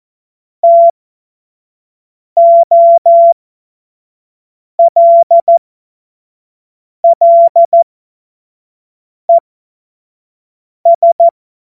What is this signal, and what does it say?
Morse code, heard as a keyed tone, about 14 wpm, spaced as for 5 wpm, 680 Hz. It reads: TOLLES
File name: Morsekurs / Witze / Witz 20. Dezember 2022 25 F70